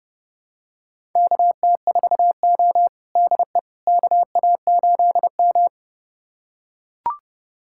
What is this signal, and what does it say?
Morse code, heard as a keyed tone, about 32 wpm, 700 Hz. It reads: KT4O DE KA8M
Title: USA Callsign DE Callsign - 4 Characters - 30wpm